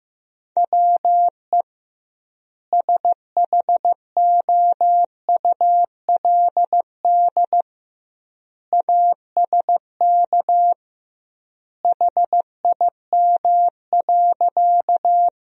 Morse code, heard as a keyed tone, about 14 wpm, 700 Hz. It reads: WE SHOULD ASK HIM.
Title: 2x Word-Spacing - Sentences from Top 300 Words - 15wpm